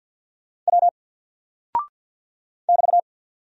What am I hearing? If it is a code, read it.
Morse code, heard as a keyed tone, about 50 wpm, 700 Hz. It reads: W <BT>
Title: Single Character (Letter:Number:Punctuation:Prosign) 50wpm